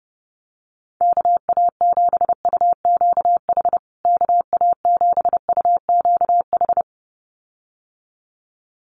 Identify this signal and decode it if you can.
Morse code, heard as a keyed tone, about 30 wpm, 700 Hz. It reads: KA7UQH KA7UQH
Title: USA Call Signs Repeated 2x 30wpm